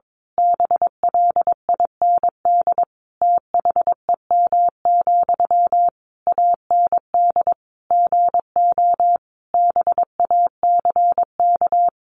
Morse code, heard as a keyed tone, about 22 wpm, 700 Hz. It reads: BLIND THEM, AND GO BACK
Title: A Sound Of Thunder 22wpm